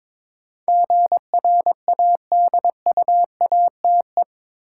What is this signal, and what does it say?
Morse code, heard as a keyed tone, about 22 wpm, 700 Hz. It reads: GRADUATE
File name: practice / 22wpm / Eight Letter Words - 22wpm